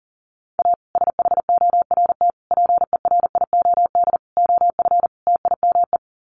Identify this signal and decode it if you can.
Morse code, heard as a keyed tone, about 40 wpm, 700 Hz. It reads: A SHORT PERIOD OF TIME
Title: Sets of 5 Words Encore 40wpm